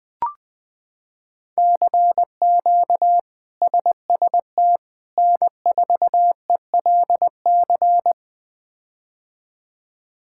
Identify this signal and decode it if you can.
Morse code, heard as a keyed tone, about 21 wpm, 700 Hz. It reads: CQ SST N4ELC
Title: K1USN SST QSO Practice - 20wpm